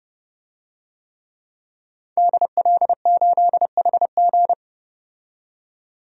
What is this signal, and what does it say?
Morse code, heard as a keyed tone, about 31 wpm, 700 Hz. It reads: DL8HG
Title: DXCC-100 Call Signs - 5 Characters - 30wpm